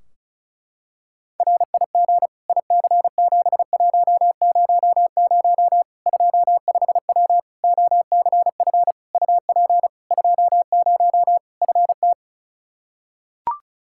Morse code, heard as a keyed tone, about 36 wpm, 700 Hz. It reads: RIG IC7100 25W OCF UP 20 FT
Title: K1USN SST QSO Practice 35wpm